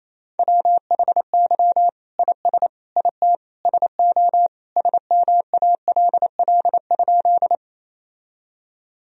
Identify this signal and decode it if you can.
Morse code, heard as a keyed tone, about 27 wpm, 700 Hz. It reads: WHY IS IT SO SMALL?
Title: Sentences from Top 1000 Words - Review Part 1 - 28wpm